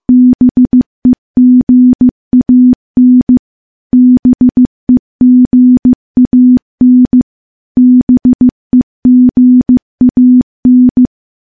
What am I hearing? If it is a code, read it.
Morse code, heard as a keyed tone, about 15 wpm, 250 Hz. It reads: BEGAN BEGAN BEGAN
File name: morse / morse-training-15WPM